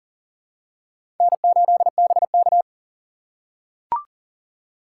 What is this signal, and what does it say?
Morse code, heard as a keyed tone, about 39 wpm, 700 Hz. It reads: N8BK
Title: USA Call Signs - 4 Characters - 3x Character Spacing - 40wpm